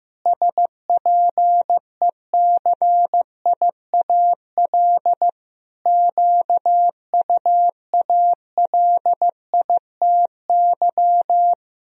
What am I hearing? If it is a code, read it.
Morse code, heard as a keyed tone, about 16 wpm, 700 Hz. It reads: SPECIAL QUALITY
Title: Sentences from Top 1000 Words - Review Part 4 - 15wpm